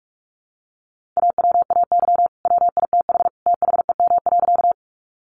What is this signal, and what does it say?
Morse code, heard as a keyed tone, about 44 wpm, 700 Hz. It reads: AWAY WITH THEM.
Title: Sentences from Top 600 Words 45wpm